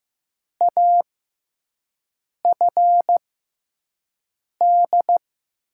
Morse code, heard as a keyed tone, about 16 wpm, spaced as for 5 wpm, 700 Hz. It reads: AFD
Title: USA Call Signs - 6x Character Spacing - 15wpm